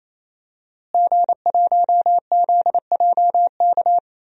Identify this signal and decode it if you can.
Morse code, heard as a keyed tone, about 28 wpm, 700 Hz. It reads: G1ZJK
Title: DXCC-100 Call Signs - 5 Characters - Repeated 3x - 28wpm